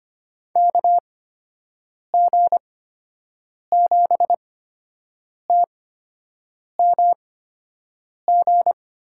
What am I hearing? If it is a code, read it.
Morse code, heard as a keyed tone, about 25 wpm, spaced as for 7 wpm, 700 Hz. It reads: KG7TMG
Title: USA Call Signs - 8x Character Spacing - 25wpm